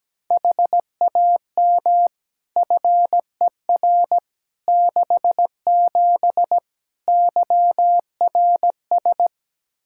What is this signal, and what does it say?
Morse code, heard as a keyed tone, about 17 wpm, 700 Hz. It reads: HAM FER 67 YRS
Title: Practice QSO - Exchange 3 - More Info - 17wpm